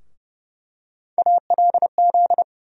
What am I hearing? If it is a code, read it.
Morse code, heard as a keyed tone, about 30 wpm, 700 Hz. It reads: ALZ
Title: USA Call Sign Postfixes - 3 Letters 30wpm